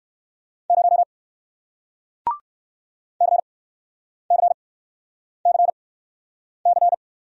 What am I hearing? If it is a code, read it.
Morse code, heard as a keyed tone, about 61 wpm, spaced as for 9 wpm, 700 Hz. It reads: 0 CCCC